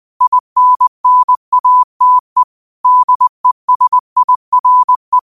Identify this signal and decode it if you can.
Morse code, heard as a keyed tone, about 20 wpm, 1000 Hz. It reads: INNATE DESIRE
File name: qotdfiles / 20wpm